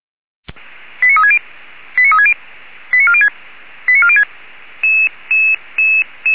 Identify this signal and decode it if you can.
Five-tone selective call (ZVEI), heard as quick runs of five tones, tones of about 70 ms. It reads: 89360 89360 89406 89406